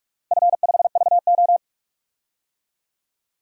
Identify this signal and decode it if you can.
Morse code, heard as a keyed tone, about 46 wpm, 700 Hz. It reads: F5VO
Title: DXCC-100 Call Signs - 4 Characters - Repeated 3x - 45wpm